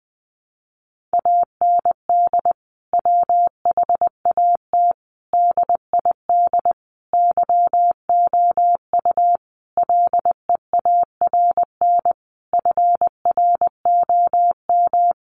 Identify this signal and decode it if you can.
Morse code, heard as a keyed tone, about 20 wpm, 700 Hz. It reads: AND WHAT DID YOU LEARN FROM
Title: Sentences from Top 300 Words 20wpm